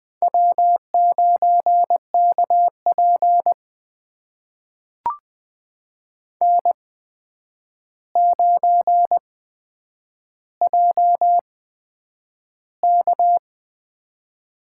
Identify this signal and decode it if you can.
Morse code, heard as a keyed tone, about 20 wpm, spaced as for 9 wpm, 700 Hz. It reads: W9KP N9JK
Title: USA Call Signs - 4 Characters - 8x Character Spacing - 20wpm